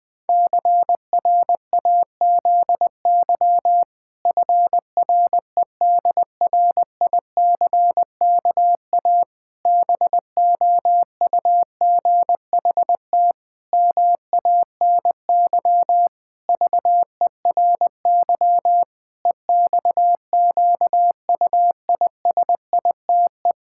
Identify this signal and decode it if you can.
Morse code, heard as a keyed tone, about 20 wpm, 700 Hz. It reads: CRAZY FREDRICKA BOUGHT MANY VERY EXQUISITE
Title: Pangram Challenge v2 - 20wpm